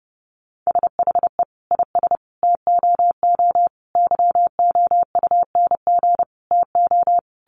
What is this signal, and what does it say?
Morse code, heard as a keyed tone, about 30 wpm, 700 Hz. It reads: SHE IS TOO YOUNG TO
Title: Sentences from Top 1300 Words 30wpm